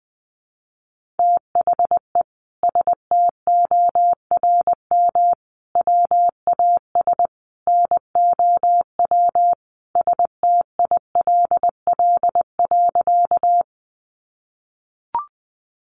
Morse code, heard as a keyed tone, about 20 wpm, 700 Hz. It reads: THE STORM WAS NOW STILL.
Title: Sentences from Top 1500 Words 20wpm